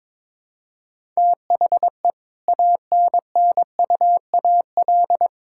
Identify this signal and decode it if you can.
Morse code, heard as a keyed tone, about 22 wpm, 700 Hz. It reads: THE ANNUAL
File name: Difficult Sentences with Hints - 22wpm